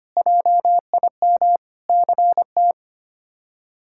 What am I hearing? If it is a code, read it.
Morse code, heard as a keyed tone, about 25 wpm, 700 Hz. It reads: JIM CT
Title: K1USN SST QSO Practice - 25wpm